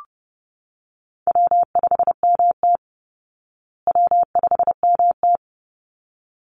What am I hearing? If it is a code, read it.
Morse code, heard as a keyed tone, about 30 wpm, 700 Hz. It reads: W5MT W5MT
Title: Easy Call Signs Repeated 2x - 30wpm